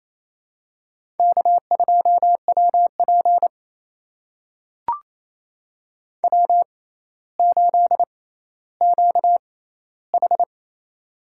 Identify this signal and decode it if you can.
Morse code, heard as a keyed tone, about 28 wpm, spaced as for 14 wpm, 700 Hz. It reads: K2WP W8QH
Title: USA Call Signs - 4 Characters - 6x Character Spacing - 28wpm